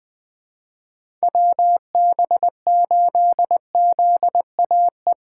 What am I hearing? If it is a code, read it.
Morse code, heard as a keyed tone, about 21 wpm, 700 Hz. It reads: WB8ZAE
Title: USA Call Signs - 6 Characters - 2x Character Spacing - 20wpm